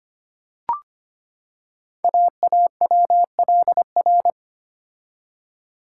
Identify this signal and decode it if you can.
Morse code, heard as a keyed tone, about 25 wpm, 700 Hz. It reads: AAWLR
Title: Code Groups - 5 Letters - 25wpm